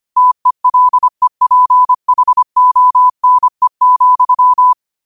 Morse code, heard as a keyed tone, about 25 wpm, 1000 Hz. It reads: TELEPHONE,
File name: qotdfiles / 25wpm